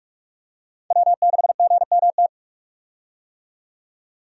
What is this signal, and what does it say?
Morse code, heard as a keyed tone, about 41 wpm, 700 Hz. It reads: W6GMT
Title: USA Call Signs - 5 Characters - 45wpm